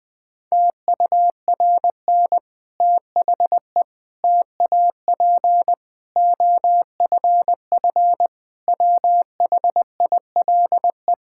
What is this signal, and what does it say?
Morse code, heard as a keyed tone, about 21 wpm, 700 Hz. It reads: TURN THE TAP OFF WHILE
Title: Difficult Sentences with Hints - 20wpm